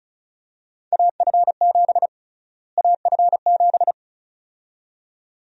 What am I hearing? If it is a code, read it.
Morse code, heard as a keyed tone, about 34 wpm, 700 Hz. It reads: AF7 AF7
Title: USA Call Sign Prefixes - Repeated 2x - 35wpm